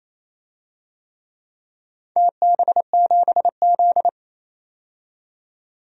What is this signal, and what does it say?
Morse code, heard as a keyed tone, about 28 wpm, 700 Hz. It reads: TB7Z
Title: DXCC-100 Call Signs - 4 Characters - 28wpm